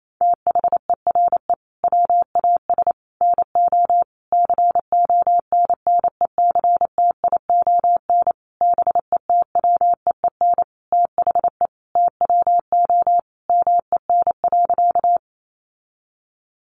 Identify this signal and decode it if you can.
Morse code, heard as a keyed tone, about 28 wpm, 700 Hz. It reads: THERE WAS NO CONNECTION BETWEEN THE TWO MEN.
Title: Sentences from Top 1300 Words 28wpm